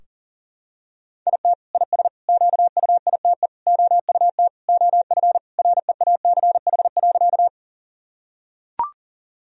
Morse code, heard as a keyed tone, about 40 wpm, 700 Hz. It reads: IT IS QUITE OUT OF REACH.
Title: Sentences from Top 1000 Words - Review Part 2 - 40wpm